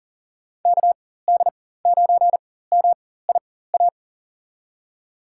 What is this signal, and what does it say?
Morse code, heard as a keyed tone, about 40 wpm, spaced as for 19 wpm, 700 Hz. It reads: KD9MIA
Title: USA Call Signs - 4x Character Spacing - 40wpm